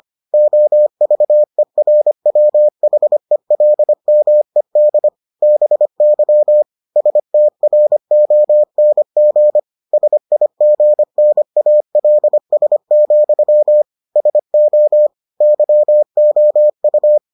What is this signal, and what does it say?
Morse code, heard as a keyed tone, about 25 wpm, 600 Hz. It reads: OVERWHELMED BY STRONG SIGNALS, SO YOU